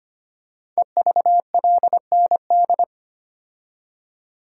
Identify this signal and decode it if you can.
Morse code, heard as a keyed tone, about 25 wpm, 700 Hz. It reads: EVLND
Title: Code Groups - 5 Letter:Numbers - 25wpm